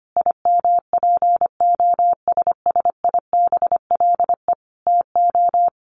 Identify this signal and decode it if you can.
Morse code, heard as a keyed tone, about 25 wpm, 700 Hz. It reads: IMPOSSIBLE TO
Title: Sentences from Top 1100 Words 25wpm